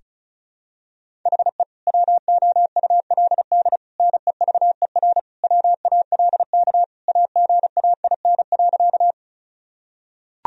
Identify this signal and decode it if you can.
Morse code, heard as a keyed tone, about 35 wpm, 700 Hz. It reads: HE WOULD NEVER WALK AGAIN.
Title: Sentences from Top 1000 Words - Review Part 1 - 35wpm